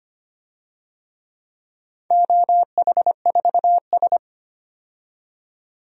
Morse code, heard as a keyed tone, about 25 wpm, 700 Hz. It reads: OH4S
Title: DXCC-100 Call Signs - 4 Characters - 25wpm